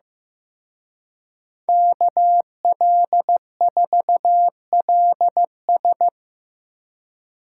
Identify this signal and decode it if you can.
Morse code, heard as a keyed tone, about 18 wpm, 700 Hz. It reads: KL4LS
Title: Speed Racing - USA 2x2 Call Signs 15wpm